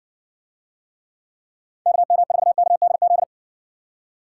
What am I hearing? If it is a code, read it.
Morse code, heard as a keyed tone, about 55 wpm, 700 Hz. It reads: KM4KDZ